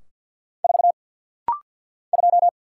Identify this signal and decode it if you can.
Morse code, heard as a keyed tone, about 50 wpm, 700 Hz. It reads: 4 2
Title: Single Letter-Number Alt - Rapid-Fire 50wpm